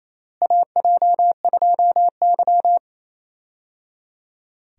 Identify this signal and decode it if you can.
Morse code, heard as a keyed tone, about 28 wpm, 700 Hz. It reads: AJ2Y
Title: USA Call Signs - 4 Characters - Repeated 3x - 28wpm